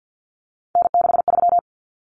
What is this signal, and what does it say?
Morse code, heard as a keyed tone, about 50 wpm, 700 Hz. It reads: N63
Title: Common Cut Numbers - 3-Digit 50wpm